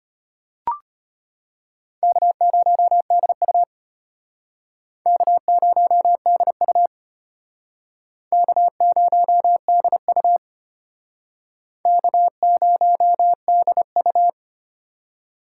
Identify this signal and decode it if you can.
Morse code, heard as a keyed tone, about 35 wpm, 700 Hz. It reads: K0DU K0DU K0DU K0DU